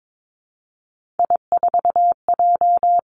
Morse code, heard as a keyed tone, about 22 wpm, 700 Hz. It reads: I4J
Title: Three Challenging Letter-Number Combinations 22wpm